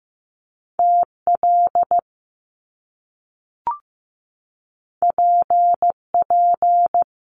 Morse code, heard as a keyed tone, about 15 wpm, 700 Hz. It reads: TL PP
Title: Top Two Letter Combinations in Words 15wpm